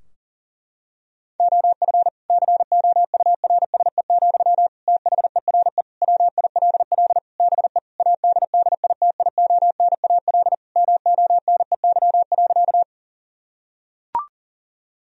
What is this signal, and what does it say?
Morse code, heard as a keyed tone, about 40 wpm, 700 Hz. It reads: OF COURSE, THERE WILL BE ADDITIONAL MONEY.
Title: Sentences from Top 1000 Words - Review Part 4 - 40wpm